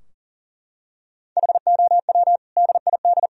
Morse code, heard as a keyed tone, about 40 wpm, 700 Hz. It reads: HOW DID
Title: Sentences from Top 1000 Words - Review Part 3 - 40wpm